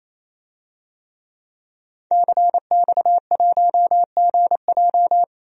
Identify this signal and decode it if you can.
Morse code, heard as a keyed tone, about 28 wpm, 700 Hz. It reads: CX1GJ
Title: DXCC-100 Call Signs - 5 Characters - 28wpm